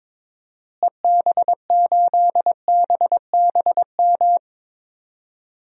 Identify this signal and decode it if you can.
Morse code, heard as a keyed tone, about 22 wpm, 700 Hz. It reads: EB8BBM
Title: DXCC-100 Call Signs - 6 Characters - Repeated 2x - 22wpm